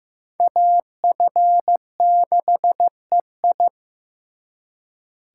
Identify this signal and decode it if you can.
Morse code, heard as a keyed tone, about 18 wpm, 700 Hz. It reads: AF6EI
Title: Speed Racing - USA 2x2 Call Signs 15wpm